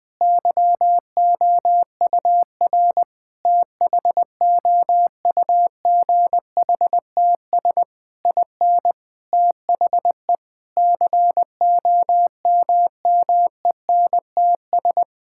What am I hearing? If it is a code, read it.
Morse code, heard as a keyed tone, about 20 wpm, 700 Hz. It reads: YOUR THOUGHTS IN THE COMMENTS
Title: Sentences with 9-Word Phrases - 20wpm